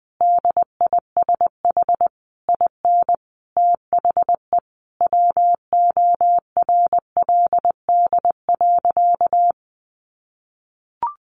Morse code, heard as a keyed tone, about 20 wpm, 700 Hz. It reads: DISH IN THE WORLD.
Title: Sentences from Top 1300 Words 20wpm